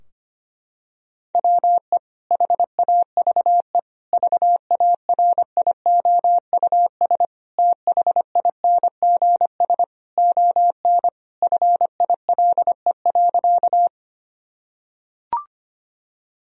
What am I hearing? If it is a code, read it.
Morse code, heard as a keyed tone, about 25 wpm, 700 Hz. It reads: WE HAVE VARIOUS THINGS ON FILE.
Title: Sentences from Top 1000 Words - Review Part 4 - 25wpm